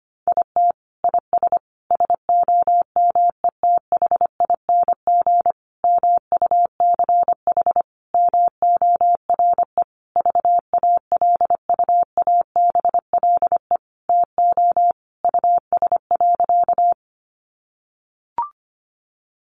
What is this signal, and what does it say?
Morse code, heard as a keyed tone, about 25 wpm, 700 Hz. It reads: IT IS SOMETHING MUCH MORE VALUABLE TO US.
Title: Sentences from Top 1100 Words 25wpm